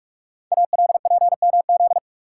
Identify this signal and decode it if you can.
Morse code, heard as a keyed tone, about 44 wpm, 700 Hz. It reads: ALPMZ
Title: Code Groups - 5 Letter:Numbers - 45wpm